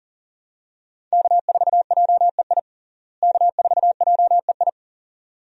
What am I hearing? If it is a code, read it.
Morse code, heard as a keyed tone, about 37 wpm, 700 Hz. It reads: K4JEI K4JEI
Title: USA Call Signs - 5 Characters - Repeated 2x - 40wpm